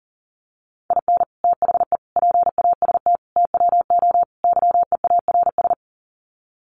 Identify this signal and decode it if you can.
Morse code, heard as a keyed tone, about 40 wpm, 700 Hz. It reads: IN THE PAST TWO YEARS
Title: Sets of 5 Words 40wpm